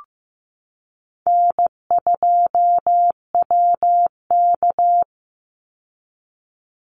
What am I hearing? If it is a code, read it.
Morse code, heard as a keyed tone, about 15 wpm, 700 Hz. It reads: N2WK
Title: Easy Call Signs Repeated 2x - 15wpm